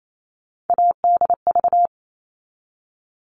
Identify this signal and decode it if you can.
Morse code, heard as a keyed tone, about 28 wpm, 700 Hz. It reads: ADV
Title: All Cut Numbers - 3-Digit 28wpm